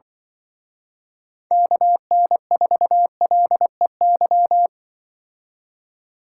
Morse code, heard as a keyed tone, about 25 wpm, 700 Hz. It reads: KN4LEY
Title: Speed Racing - USA 2x3 Call Signs 20wpm